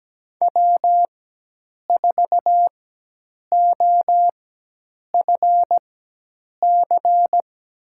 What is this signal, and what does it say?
Morse code, heard as a keyed tone, about 17 wpm, spaced as for 8 wpm, 700 Hz. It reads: W4OFC
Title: USA Call Signs - 5 Characters - 4x Character Spacing - 17wpm